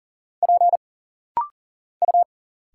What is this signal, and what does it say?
Morse code, heard as a keyed tone, about 40 wpm, 700 Hz. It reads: P U
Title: Single Letters - Rapid-Fire 40wpm